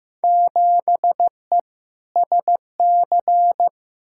Morse code, heard as a keyed tone, about 15 wpm, 700 Hz. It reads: 7E SC
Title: Field Day Exchanges Encore - 15wpm